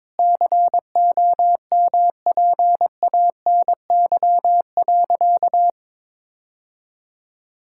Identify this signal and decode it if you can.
Morse code, heard as a keyed tone, about 22 wpm, 700 Hz. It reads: COMPANY.
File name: practice / 22wpm / Sentences with 2-Word Phrases - 22wpm